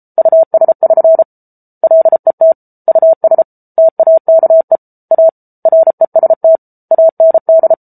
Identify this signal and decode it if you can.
Morse code, heard as a keyed tone, about 34 wpm, 650 Hz. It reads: US<SN> LET US TAKE A REST AND